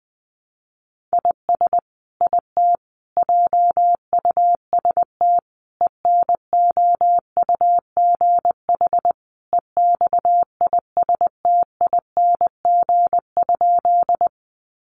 Morse code, heard as a keyed tone, about 20 wpm, 700 Hz. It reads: IS IT JUST ENOUGH EXISTING?
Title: Sentences from Top 1500 Words 20wpm